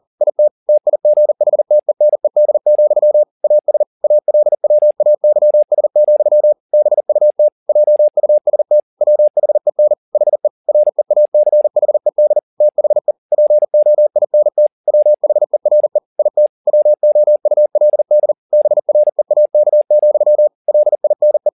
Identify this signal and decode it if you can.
Morse code, heard as a keyed tone, about 40 wpm, 600 Hz. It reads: IT TIGHTENED, AS ALWAYS, BUT JUST WHEN HE REACHED THE POINT WHERE IT WOULD BREAK, LINE
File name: the old man and the sea 2 40wpm